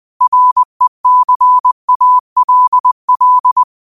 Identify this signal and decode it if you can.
Morse code, heard as a keyed tone, about 20 wpm, 1000 Hz. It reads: RECALL
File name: qotdfiles / 20wpm